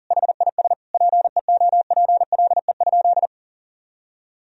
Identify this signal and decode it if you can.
Morse code, heard as a keyed tone, about 40 wpm, 700 Hz. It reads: HIS PEOPLE?
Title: Sentences from Top 1000 Words - Review Part 3 - 40wpm